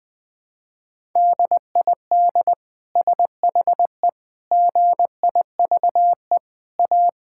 Morse code, heard as a keyed tone, about 20 wpm, 700 Hz. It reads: DID SHE GIVE A
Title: Sentences from Top 1000 Words - Review Part 6 - 20wpm